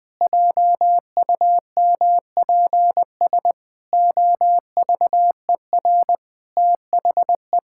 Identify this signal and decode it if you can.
Morse code, heard as a keyed tone, about 20 wpm, 700 Hz. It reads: JUMPS OVER THE
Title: Sets of 8 Words - 20wpm